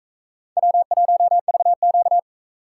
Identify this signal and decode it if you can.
Morse code, heard as a keyed tone, about 39 wpm, 700 Hz. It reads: W1VQ